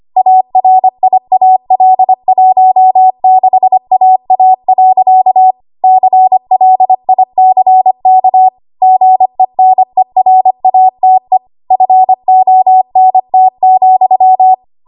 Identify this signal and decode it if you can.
Morse code, heard as a keyed tone, about 25 wpm, 750 Hz. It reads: ARIAL16AA. CLICK GENERATE FONT,
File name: Archive / 25 WPM / 25WPM